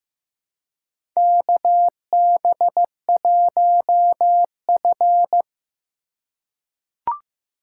Morse code, heard as a keyed tone, about 15 wpm, 700 Hz. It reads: KB1F
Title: USA Call Signs - 4 Characters - 4x Character Spacing - 15wpm